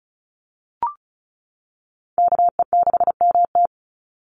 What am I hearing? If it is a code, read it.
Morse code, heard as a keyed tone, about 34 wpm, 700 Hz. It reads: KE6MT
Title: NA SOTA Chaser Callsigns 35wpm